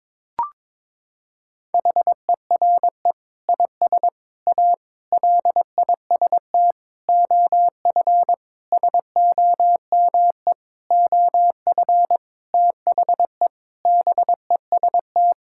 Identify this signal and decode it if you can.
Morse code, heard as a keyed tone, about 22 wpm, 700 Hz. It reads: HERE IS A LIST OF SOME OF THE BEST
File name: practice / 22wpm / Sentences with 8-Word Phrases - 22wpm